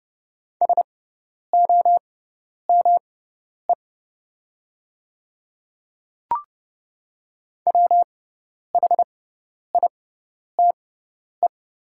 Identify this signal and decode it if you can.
Morse code, heard as a keyed tone, about 30 wpm, spaced as for 10 wpm, 700 Hz. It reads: SOME WHITE
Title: 6x Character Spacing - Top 300 Words - 30wpm